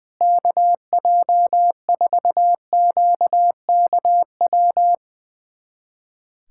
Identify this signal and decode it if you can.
Morse code, heard as a keyed tone, about 20 wpm, 700 Hz. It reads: KJ4QKW
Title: USA Callsign DE Callsign 20wpm